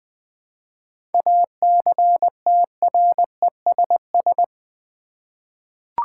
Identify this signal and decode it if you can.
Morse code, heard as a keyed tone, about 20 wpm, 700 Hz. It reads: ACTRESS
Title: 7 Letter Words - First 3 Letters Given - 20wpm